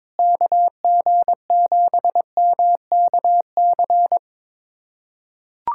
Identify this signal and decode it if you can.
Morse code, heard as a keyed tone, about 22 wpm, 700 Hz. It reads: KG7MKC
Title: USA Callsign DE Callsign - 6 Characters - 22wpm